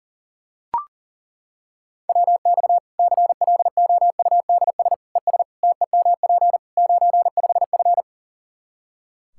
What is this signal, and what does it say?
Morse code, heard as a keyed tone, about 40 wpm, 700 Hz. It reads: WX CLOUDS ES TEMP 95F
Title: Practice QSO - Exchange 2 - Info - 40wpm